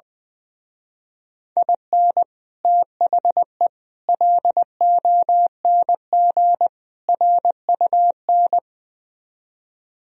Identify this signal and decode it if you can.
Morse code, heard as a keyed tone, about 20 wpm, 700 Hz. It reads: IN THE LONG RUN
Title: Speed-Racing - Sets of 4 Words 20wpm